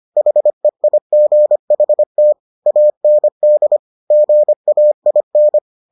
Morse code, heard as a keyed tone, about 25 wpm, 600 Hz. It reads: HEIGHT AND GAIN